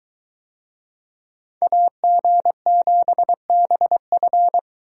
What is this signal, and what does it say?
Morse code, heard as a keyed tone, about 21 wpm, 700 Hz. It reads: AG7BF